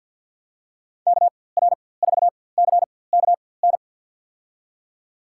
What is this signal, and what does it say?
Morse code, heard as a keyed tone, about 53 wpm, spaced as for 24 wpm, 700 Hz. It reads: KR4CKN
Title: USA Call Signs - 6 Characters - 4x Character Spacing - 50wpm